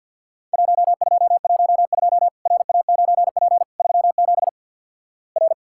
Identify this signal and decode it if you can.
Morse code, heard as a keyed tone, about 51 wpm, 700 Hz. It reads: 1212 RA9P 37 R
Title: Worked All Europe DX Contest - QTC Exchange with Repeats - 50wpm